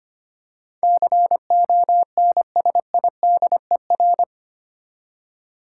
Morse code, heard as a keyed tone, about 25 wpm, 700 Hz. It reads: CONSIDER
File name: Eight Letter Words - 25wpm